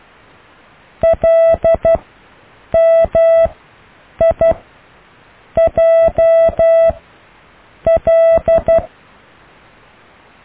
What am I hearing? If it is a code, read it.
Morse code, heard as a keyed tone, about 12 wpm, 660 Hz. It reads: L M I J L
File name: cw / IJLMU